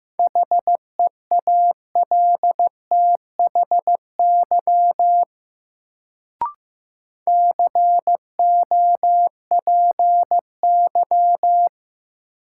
Morse code, heard as a keyed tone, about 15 wpm, 700 Hz. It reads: HEALTHY COPY
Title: Sentences from Top 1400 Words - New Word Drill Alt - 15wpm